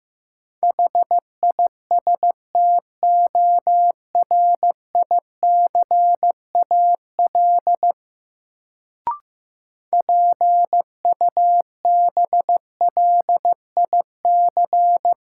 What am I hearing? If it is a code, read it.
Morse code, heard as a keyed tone, about 15 wpm, 700 Hz. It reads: HISTORICAL PUBLIC
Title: Sentences from Top 600 Words - New Word Drill - 15wpm